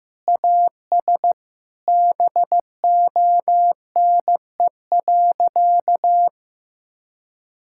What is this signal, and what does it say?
Morse code, heard as a keyed tone, about 15 wpm, 700 Hz. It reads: AS BONE.
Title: Sentences from Top 1600 Words 15wpm